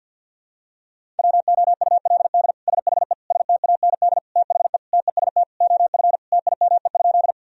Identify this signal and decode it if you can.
Morse code, heard as a keyed tone, about 50 wpm, 700 Hz. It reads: WOULD SHE STAND THE TEST OF TIME?
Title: Sentences from Top 1000 Words - Review Part 3 - 50wpm